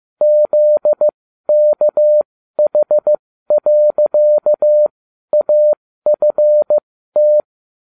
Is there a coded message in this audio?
Yes: Morse code from 0.2 to 7.4 s, heard as a keyed tone, about 15 wpm, spaced as for 12 wpm, 600 Hz. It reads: ZKH.AFT